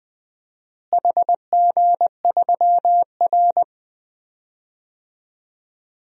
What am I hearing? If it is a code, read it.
Morse code, heard as a keyed tone, about 20 wpm, 700 Hz. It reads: HG3R
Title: DXCC-100 Call Signs - 4 Characters - Repeated 2x - 20wpm